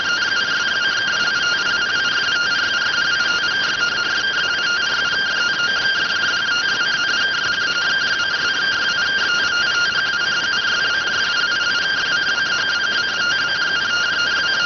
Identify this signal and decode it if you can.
Two-tone teleprinter signal, about 170 Hz shift.